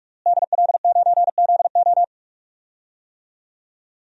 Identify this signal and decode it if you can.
Morse code, heard as a keyed tone, about 41 wpm, 700 Hz. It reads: DL9ZO
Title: DXCC-100 Call Signs - 5 Characters - Repeated 2x - 45wpm